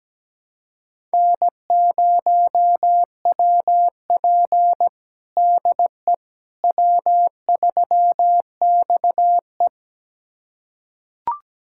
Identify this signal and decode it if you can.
Morse code, heard as a keyed tone, about 17 wpm, 700 Hz. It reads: N0WP DE W3XE
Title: USA Callsign DE Callsign - 4 Characters - 17wpm